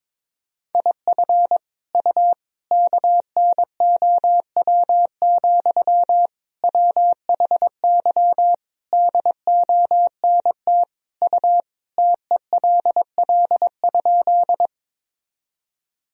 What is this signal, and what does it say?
Morse code, heard as a keyed tone, about 22 wpm, 700 Hz. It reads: IF U KNOW, WHY DONT U TELL?
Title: Sentences from Top 300 Words and Abbreviations 22wpm